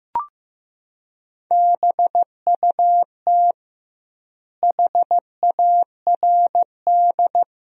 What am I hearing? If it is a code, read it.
Morse code, heard as a keyed tone, about 16 wpm, 700 Hz. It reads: BUT HARD
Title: 2x Word-Spacing - Sentences from Top 300 Words - 15wpm